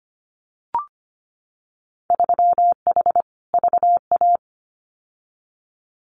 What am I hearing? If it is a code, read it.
Morse code, heard as a keyed tone, about 25 wpm, 700 Hz. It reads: 3H VA
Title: Winter Field Day Exchanges 25wpm